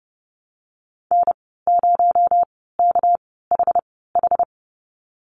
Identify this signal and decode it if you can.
Morse code, heard as a keyed tone, about 30 wpm, spaced as for 17 wpm, 700 Hz. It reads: N0KHH
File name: Lesson 37g - Letter x - Call Signs 30wpm